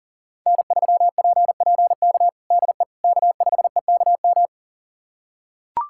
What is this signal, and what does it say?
Morse code, heard as a keyed tone, about 40 wpm, 700 Hz. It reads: N3PPK DE K5EKM
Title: USA Callsign DE Callsign - 5 Characters - 40wpm